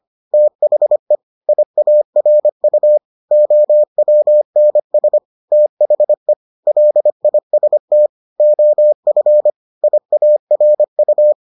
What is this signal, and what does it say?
Morse code, heard as a keyed tone, about 25 wpm, 600 Hz. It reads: THE IARU OWNS THE LIST OF IARU